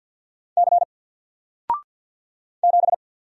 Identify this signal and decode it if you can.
Morse code, heard as a keyed tone, about 48 wpm, 700 Hz. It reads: C 7
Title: Single Letter-Number 50wpm